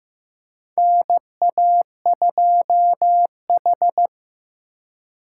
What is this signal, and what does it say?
Morse code, heard as a keyed tone, about 15 wpm, 700 Hz. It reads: NA2H
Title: USA Call Signs - 4 Characters - 2x Character Spacing - 15wpm